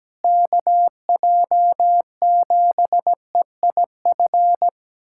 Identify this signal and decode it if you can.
Morse code, heard as a keyed tone, about 17 wpm, 700 Hz. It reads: KJ7EIF
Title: USA Callsign DE Callsign 17wpm